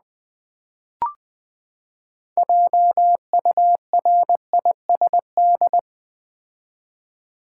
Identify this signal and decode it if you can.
Morse code, heard as a keyed tone, about 20 wpm, 700 Hz. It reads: JURISD
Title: Build A Word - 6 Letters At A Time - 12-Letter Words - 20wpm